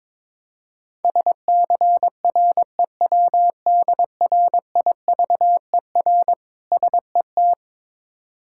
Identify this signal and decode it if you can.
Morse code, heard as a keyed tone, about 22 wpm, 700 Hz. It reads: SCREWDRIVER SET
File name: Difficult Words with Hints - 22wpm